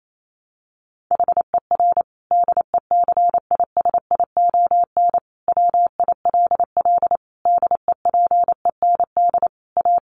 Morse code, heard as a keyed tone, about 28 wpm, 700 Hz. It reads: HER DECISION WILL DEPEND A
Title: Sentences from Top 1300 Words 28wpm